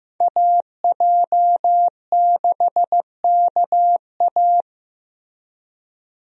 Morse code, heard as a keyed tone, about 15 wpm, 700 Hz. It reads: AJ6KA
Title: USA Call Signs - 5 Characters - Repeated 2x - 15wpm